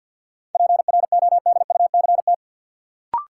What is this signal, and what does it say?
Morse code, heard as a keyed tone, about 53 wpm, 700 Hz. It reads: PRODUCT
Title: Sentences from Top 1400 Words - New Word Drill Alt - 50wpm